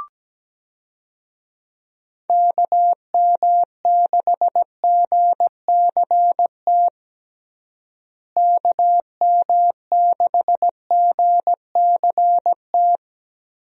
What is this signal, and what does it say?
Morse code, heard as a keyed tone, about 17 wpm, 700 Hz. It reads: KM6GCT KM6GCT
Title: USA Call Signs - 6 Characters - Repeated 2x - 17wpm